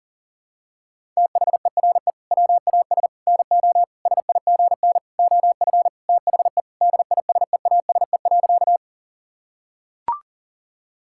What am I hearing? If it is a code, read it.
Morse code, heard as a keyed tone, about 40 wpm, 700 Hz. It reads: THERE WAS NO SIGN OF THE DISEASE.
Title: Sentences from Top 1000 Words - Review Part 6 - 40wpm